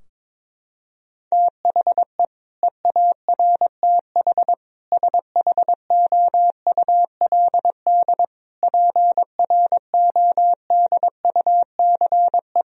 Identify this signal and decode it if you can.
Morse code, heard as a keyed tone, about 22 wpm, 700 Hz. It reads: THE EARTH SHOULD PRODUCE
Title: Sentences from Top 1000 Words - Review Part 3 - 22wpm